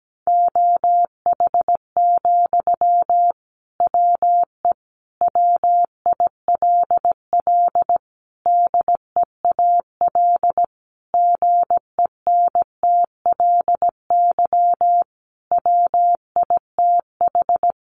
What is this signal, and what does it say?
Morse code, heard as a keyed tone, about 17 wpm, 700 Hz. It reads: OH, WE WILL DEAL GENTLY WITH
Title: Sentences from Top 1500 Words 17wpm